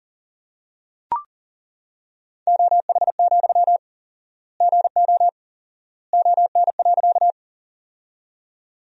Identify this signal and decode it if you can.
Morse code, heard as a keyed tone, about 40 wpm, 700 Hz. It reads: OH, GO ON.
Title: 4x Word-Spacing - Sentences from Top 500 Words - 40wpm